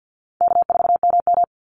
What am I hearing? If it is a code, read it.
Morse code, heard as a keyed tone, about 50 wpm, 700 Hz. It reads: K4MM
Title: Call Signs - 50wpm